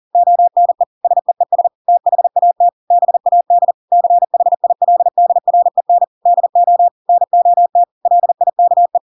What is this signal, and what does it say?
Morse code, heard as a keyed tone, about 40 wpm, 700 Hz. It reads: ONE SEES THAT BAD CHILDREN DO NOT LIKE
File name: Pinocchio / Pinocchio ch4 40 40 ik1hge